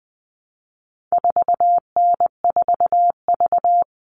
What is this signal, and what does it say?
Morse code, heard as a keyed tone, about 20 wpm, 700 Hz. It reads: 4N4V
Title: All Cut Numbers - 4-Digit 20wpm